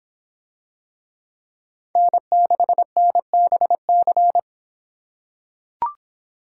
Morse code, heard as a keyed tone, about 26 wpm, 700 Hz. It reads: N6NBC